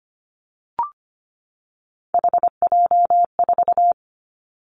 Morse code, heard as a keyed tone, about 25 wpm, 700 Hz. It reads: HJ4
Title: Callsign Prefixes for Top 100 DXCC - 25wpm